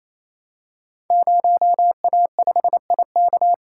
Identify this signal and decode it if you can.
Morse code, heard as a keyed tone, about 28 wpm, 700 Hz. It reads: 0A5IK
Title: Code Groups - 5 Letter:Numbers - 28wpm